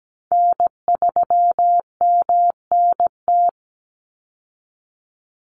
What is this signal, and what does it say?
Morse code, heard as a keyed tone, about 17 wpm, 700 Hz. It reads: N3MNT
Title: Call Signs Repeated 2x - 17wpm